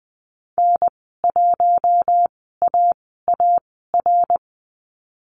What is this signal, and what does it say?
Morse code, heard as a keyed tone, about 20 wpm, spaced as for 15 wpm, 700 Hz. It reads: N1AAR
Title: 088 Lesson 15g - Number 5 - Call Signs 20wpm